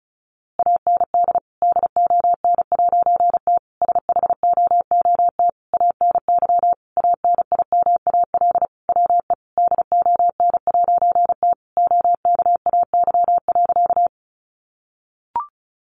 Morse code, heard as a keyed tone, about 34 wpm, 700 Hz. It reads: AND DON'T SHOOT ANY ANIMAL WE DON'T OKAY.
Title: A Sound Of Thunder 35wpm